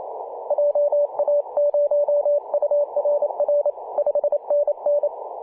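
Morse code, heard as a keyed tone, about 28 wpm, 580 Hz. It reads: JA0URR 5NN
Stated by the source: P4/DL6RAI heard on CW on 7.003 MHz, recorded by ja0rug.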